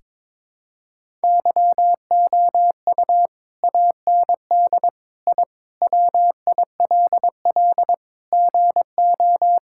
Morse code, heard as a keyed tone, about 22 wpm, 700 Hz. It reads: YOU AND I WILL GO
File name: practice / 22wpm / Sentences with 2-Word Phrases - 22wpm